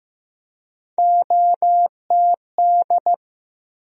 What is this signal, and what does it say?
Morse code, heard as a keyed tone, about 15 wpm, 700 Hz. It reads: OTD
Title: USA Call Sign Postfixes - 3 Letters 15wpm